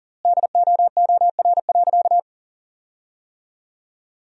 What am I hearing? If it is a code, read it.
Morse code, heard as a keyed tone, about 40 wpm, 700 Hz. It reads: DOOR.
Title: 3x Word-Spacing - Sentences from Top 500 Words - 40wpm